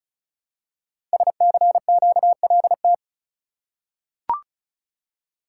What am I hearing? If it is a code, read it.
Morse code, heard as a keyed tone, about 36 wpm, 700 Hz. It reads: SCQLT
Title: Code Groups - 5 Letters - 35wpm